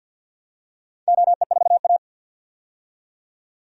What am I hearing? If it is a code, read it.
Morse code, heard as a keyed tone, about 50 wpm, 700 Hz. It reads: OE4A
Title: DXCC-100 Call Signs - 4 Characters - 50wpm